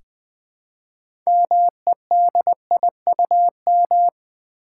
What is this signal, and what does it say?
Morse code, heard as a keyed tone, about 20 wpm, 690 Hz. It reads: MEDIUM
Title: Build A Word - 2 Letters At A Time - 6-Letter Words - 20wpm